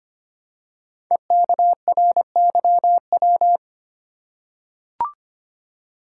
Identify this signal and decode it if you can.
Morse code, heard as a keyed tone, about 25 wpm, 700 Hz. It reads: EKRYW
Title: Code Groups - 5 Letters - 25wpm